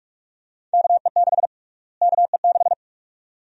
Morse code, heard as a keyed tone, about 43 wpm, 700 Hz. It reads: KE6 KE6
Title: USA Call Sign Prefixes - Repeated 2x - 45wpm